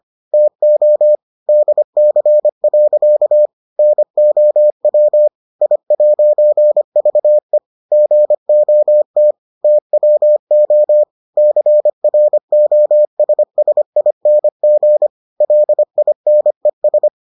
Morse code, heard as a keyed tone, about 25 wpm, 600 Hz. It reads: TO DC. NOW I'VE GOT TWO CROSSING LINES